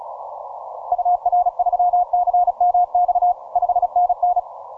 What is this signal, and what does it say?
Morse code, heard as a keyed tone, about 35 wpm, 710 Hz. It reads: UR3CMX 5NN